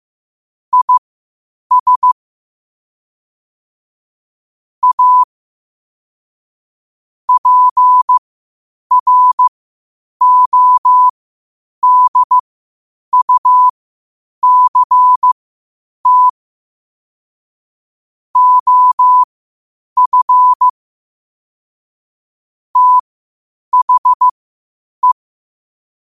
Morse code, heard as a keyed tone, about 15 wpm, spaced as for 8 wpm, 1000 Hz. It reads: IS A PRODUCT OF THE
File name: qotdfiles / 8wpm